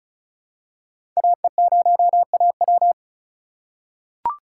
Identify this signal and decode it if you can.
Morse code, heard as a keyed tone, about 36 wpm, 700 Hz. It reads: AE0AW
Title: USA Call Signs - 8x Character Spacing - 35wpm